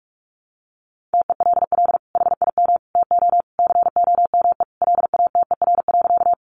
Morse code, heard as a keyed tone, about 44 wpm, 700 Hz. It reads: TELL HIM TO COME LATER.
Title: Sentences from Top 700 Words 45wpm